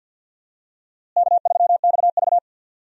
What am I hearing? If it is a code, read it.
Morse code, heard as a keyed tone, about 52 wpm, 700 Hz. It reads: K3XV
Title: USA Call Signs - 4 Characters - Repeated 2x - 50wpm